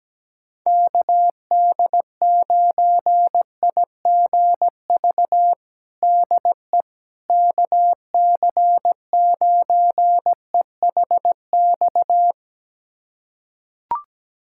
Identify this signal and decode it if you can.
Morse code, heard as a keyed tone, about 17 wpm, 700 Hz. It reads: KD9IGV DE KC9EHX
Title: USA Callsign DE Callsign - 6 Characters - 17wpm